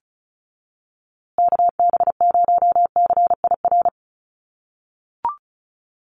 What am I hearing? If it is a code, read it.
Morse code, heard as a keyed tone, about 34 wpm, 700 Hz. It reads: KB0CIR
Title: USA Call Signs Repeated 3x 35wpm